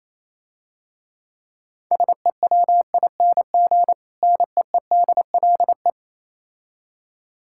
Morse code, heard as a keyed tone, about 27 wpm, 700 Hz. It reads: SEWING NEEDLE
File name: Easy Words - Speed-Racing with Hints - 22wpm